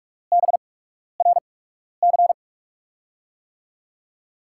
Morse code, heard as a keyed tone, about 44 wpm, spaced as for 12 wpm, 700 Hz. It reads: BRC
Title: USA Call Signs - 8x Character Spacing - 45wpm